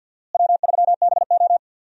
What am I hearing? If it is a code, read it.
Morse code, heard as a keyed tone, about 47 wpm, 700 Hz. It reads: W3BO
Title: USA Call Signs - 4 Characters - Repeated 2x - 50wpm